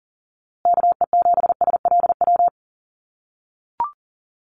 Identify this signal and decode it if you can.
Morse code, heard as a keyed tone, about 40 wpm, 700 Hz. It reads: KE7SLW
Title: USA Call Signs Repeated 2x 40wpm